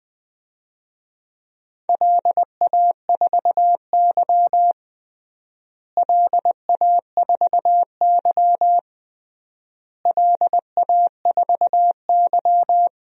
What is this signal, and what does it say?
Morse code, heard as a keyed tone, about 21 wpm, 700 Hz. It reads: LA4Y LA4Y LA4Y
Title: DXCC-100 Call Signs - 4 Characters - Repeated 3x - 20wpm